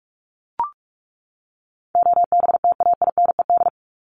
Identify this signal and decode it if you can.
Morse code, heard as a keyed tone, about 44 wpm, 700 Hz. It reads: OBTAINED
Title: ED Suffix - Top 100 Words with Base Words 45wpm